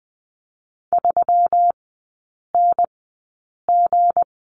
Morse code, heard as a keyed tone, about 20 wpm, spaced as for 8 wpm, 690 Hz. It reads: 3NG
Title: 166 Lesson 27e - Number 7 - Three Characters 20wpm